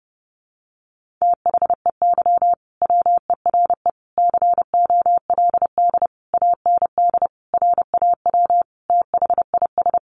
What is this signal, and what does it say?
Morse code, heard as a keyed tone, about 30 wpm, 700 Hz. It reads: THEY WERE COLD AND RAW THIS
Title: Sentences from Top 1500 Words 30wpm